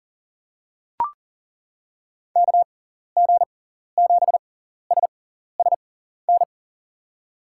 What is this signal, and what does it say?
Morse code, heard as a keyed tone, about 40 wpm, spaced as for 14 wpm, 700 Hz. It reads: KG7SSN
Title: USA Call Signs - 6x Character Spacing - 40wpm